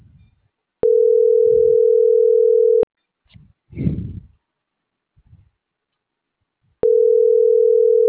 Ringback tone (North America), 440 + 480 Hz.